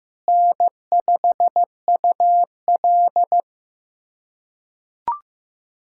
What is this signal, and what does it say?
Morse code, heard as a keyed tone, about 15 wpm, 700 Hz. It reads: N5UL
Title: USA Callsign DE Callsign - 4 Characters - 15wpm